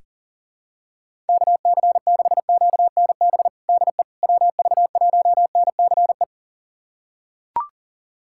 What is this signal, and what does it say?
Morse code, heard as a keyed tone, about 40 wpm, 700 Hz. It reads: KC6QNB DE WV1NCE
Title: USA Callsign DE Callsign - 6 Characters - 40wpm